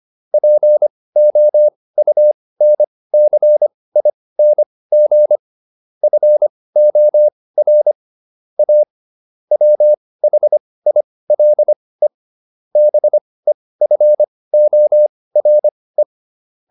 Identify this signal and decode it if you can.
Morse code, heard as a keyed tone, about 25 wpm, spaced as for 18 wpm, 600 Hz. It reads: POUNCING FOR A WHILE BEFORE